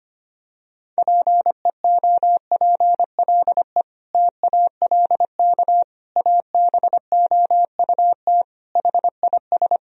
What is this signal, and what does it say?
Morse code, heard as a keyed tone, about 25 wpm, 700 Hz. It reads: PEOPLE TALK ABOUT HIS
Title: Sentences from Top 1000 Words - Review Part 3 - 25wpm